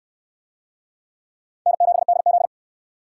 Morse code, heard as a keyed tone, about 68 wpm, 700 Hz. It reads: N7DZ